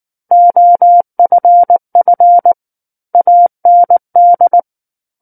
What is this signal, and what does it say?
Morse code, heard as a keyed tone, about 19 wpm, 700 Hz. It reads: OFF AND
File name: Book One / 10-Chapter 10 - 19 WPM